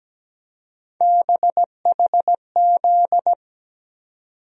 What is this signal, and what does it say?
Morse code, heard as a keyed tone, about 17 wpm, 700 Hz. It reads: BHZ
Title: USA Call Sign Postfixes - 3 Letters 17wpm